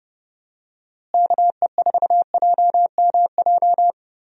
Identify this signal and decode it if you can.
Morse code, heard as a keyed tone, about 30 wpm, 700 Hz. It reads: KE4JMJ
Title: USA Call Signs - 6 Characters - Repeated 2x - 30wpm